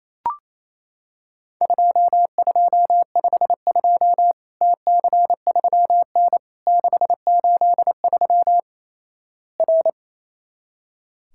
Morse code, heard as a keyed tone, about 27 wpm, 700 Hz. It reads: 2252 TC3N 683 R
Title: Worked All Europe DX Contest - QTC Exchange - 28wpm